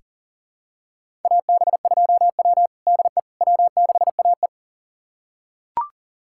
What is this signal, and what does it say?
Morse code, heard as a keyed tone, about 40 wpm, 700 Hz. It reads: AB2W DE W6AE
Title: USA Callsign DE Callsign - 4 Characters - 40wpm